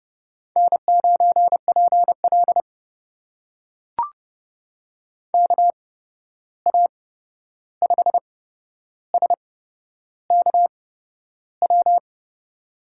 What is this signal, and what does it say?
Morse code, heard as a keyed tone, about 30 wpm, spaced as for 11 wpm, 700 Hz. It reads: N9PL KA5SKW
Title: USA Call Signs - 8x Character Spacing - 30wpm